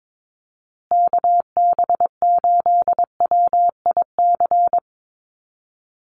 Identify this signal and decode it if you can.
Morse code, heard as a keyed tone, about 22 wpm, 700 Hz. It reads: KB8WIC
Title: USA Call Signs Repeated 2x 22wpm